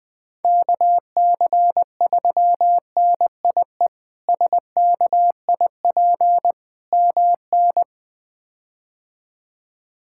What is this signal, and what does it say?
Morse code, heard as a keyed tone, about 21 wpm, 700 Hz. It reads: KC3NIE SKIP MN
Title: K1USN SST QSO Practice - 20wpm